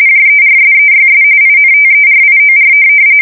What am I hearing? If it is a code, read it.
RTTY, about 45.45 baud. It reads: JERSEYS INTERIOR
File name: fsk170 / USS-New-Jersey